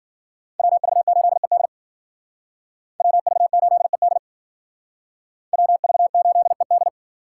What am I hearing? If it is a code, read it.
Morse code, heard as a keyed tone, about 52 wpm, 700 Hz. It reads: WV8ED WV8ED WV8ED